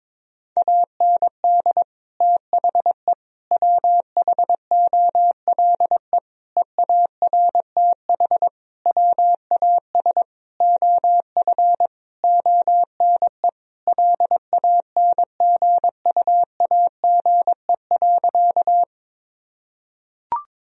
Morse code, heard as a keyed tone, about 22 wpm, 700 Hz. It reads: AND THE WHOLE EARTH WAS OF ONE LANGUAGE.
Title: Sentences from Top 1000 Words - Review Part 3 - 22wpm